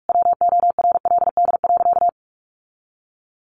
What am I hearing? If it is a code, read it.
Morse code, heard as a keyed tone, about 44 wpm, 700 Hz. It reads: WORLD.
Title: Sentences from Top 1500 Words 45wpm